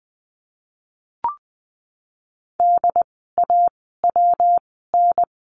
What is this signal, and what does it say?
Morse code, heard as a keyed tone, about 20 wpm, spaced as for 14 wpm, 700 Hz. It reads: DAWN
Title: 115 Lesson 19f - Letter w - Words 20wpm